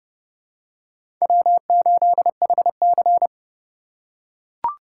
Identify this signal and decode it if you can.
Morse code, heard as a keyed tone, about 32 wpm, 700 Hz. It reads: W8HC
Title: USA Call Signs - 4 Characters - 3x Character Spacing - 30wpm